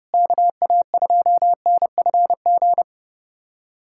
Morse code, heard as a keyed tone, about 30 wpm, 700 Hz. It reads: KA2NFG
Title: USA Callsign DE Callsign - 6 Characters - 30wpm